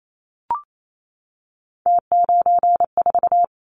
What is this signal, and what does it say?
Morse code, heard as a keyed tone, about 28 wpm, 700 Hz. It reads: T94
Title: Callsign Prefixes for Top 100 DXCC - 28wpm